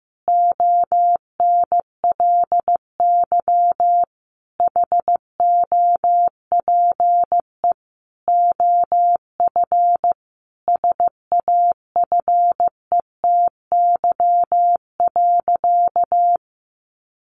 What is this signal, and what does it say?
Morse code, heard as a keyed tone, about 15 wpm, 700 Hz. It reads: ONLY HOPE OF SAFETY.
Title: Sentences from Top 1000 Words 15wpm